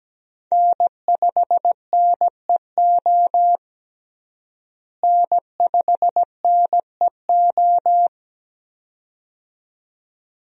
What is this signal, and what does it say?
Morse code, heard as a keyed tone, about 17 wpm, 700 Hz. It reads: N5NEO N5NEO
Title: USA Call Signs - 5 Characters - Repeated 2x - 17wpm